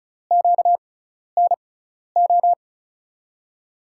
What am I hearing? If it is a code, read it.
Morse code, heard as a keyed tone, about 36 wpm, spaced as for 12 wpm, 700 Hz. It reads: QNO
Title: USA Call Signs - 6x Character Spacing - 35wpm